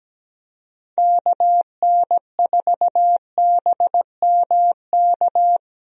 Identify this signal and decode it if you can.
Morse code, heard as a keyed tone, about 17 wpm, 700 Hz. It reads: KN4BMK
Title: USA Call Signs - 6 Characters - Repeated 3x - 17wpm